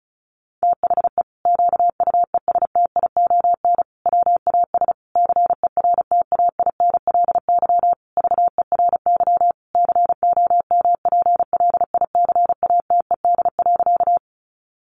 Morse code, heard as a keyed tone, about 34 wpm, 700 Hz. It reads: THE QUESTION WAS CERTAINLY VERY COMPLICATED.
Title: Sentences from Top 1200 Words 35wpm